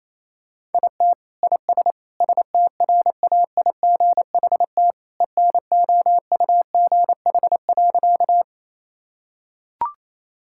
Morse code, heard as a keyed tone, about 28 wpm, 700 Hz. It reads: IT IS STRAIGHT ENOUGH.
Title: Sentences from Top 1000 Words - Review Part 6 - 28wpm